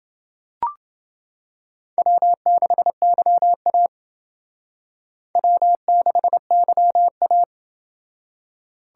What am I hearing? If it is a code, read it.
Morse code, heard as a keyed tone, about 29 wpm, 700 Hz. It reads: W6YA W6YA